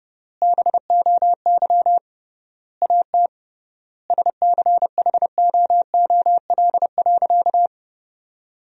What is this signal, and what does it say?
Morse code, heard as a keyed tone, about 30 wpm, 700 Hz. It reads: BOY AT SCHOOL.
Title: 3x Word-Spacing - Sentences from Top 300 Words - 30wpm